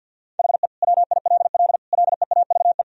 Morse code, heard as a keyed tone, about 49 wpm, 700 Hz. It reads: HE WILL LEAVE
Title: Sentences from Top 1000 Words - Review Part 1 - 50wpm